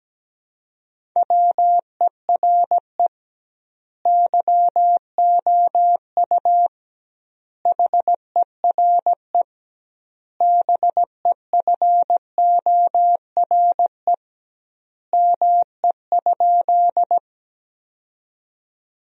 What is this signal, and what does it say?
Morse code, heard as a keyed tone, about 17 wpm, 700 Hz. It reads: WERE YOU HERE BEFORE ME?
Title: 2x Word-Spacing - Sentences from Top 200 Words - 17wpm